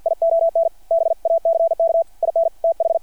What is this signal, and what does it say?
Morse code, heard as a keyed tone, about 50 wpm, 720 Hz. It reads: ION BACK IN TH
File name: archived / CRSnet